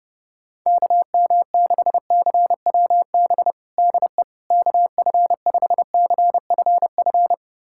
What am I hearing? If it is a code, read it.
Morse code, heard as a keyed tone, about 30 wpm, 700 Hz. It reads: KM6CWB DE KF5CFF
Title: USA Callsign DE Callsign 30wpm